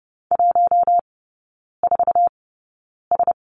Morse code, heard as a keyed tone, about 30 wpm, spaced as for 9 wpm, 700 Hz. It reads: 14S
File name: course / 30:10wpm / 087 Lesson 15e - Number 5 - Three Characters 30wpm